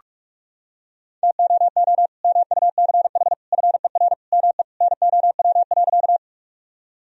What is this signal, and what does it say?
Morse code, heard as a keyed tone, about 41 wpm, 700 Hz. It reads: TOO MUCH FER ME NOW.
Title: Sentences from Top 200 Words and Abbreviations 45wpm